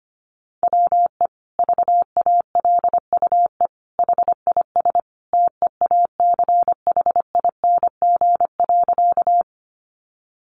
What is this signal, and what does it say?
Morse code, heard as a keyed tone, about 25 wpm, 700 Hz. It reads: WE VALUE HIS TEACHING.
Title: Sentences from Top 1100 Words 25wpm